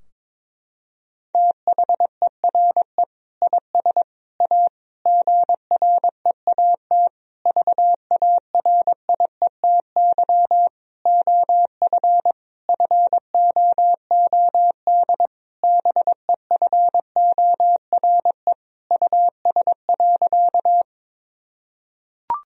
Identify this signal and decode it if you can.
Morse code, heard as a keyed tone, about 22 wpm, 700 Hz. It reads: THERE IS A GREAT VARIETY OF FOOD BEFORE US.
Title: Sentences from Top 1000 Words - Review Part 4 - 22wpm